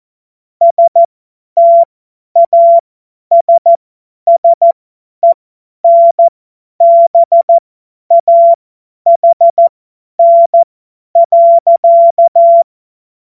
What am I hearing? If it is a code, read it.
Morse code, heard as a keyed tone, about 14 wpm, spaced as for 10 wpm, 680 Hz. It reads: STASSENBAHN.
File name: Morsekurs / Witze / Witz 15. November 2022 50 F70